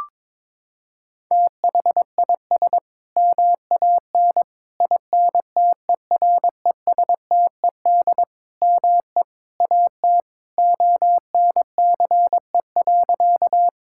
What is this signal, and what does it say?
Morse code, heard as a keyed tone, about 22 wpm, 700 Hz. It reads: THIS MAN INTERESTED ME AT ONCE.
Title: Sentences from Top 1000 Words - Review Part 6 - 22wpm